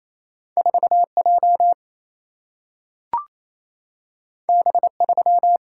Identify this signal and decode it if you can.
Morse code, heard as a keyed tone, about 28 wpm, 700 Hz. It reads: 4J B3
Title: Single-Letter Single-Number 28wpm